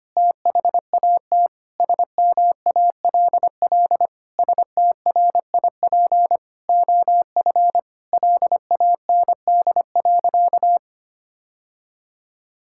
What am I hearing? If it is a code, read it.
Morse code, heard as a keyed tone, about 25 wpm, 700 Hz. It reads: THAT SMALL STRIP OF LAND.
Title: Sentences from Top 1600 Words 25wpm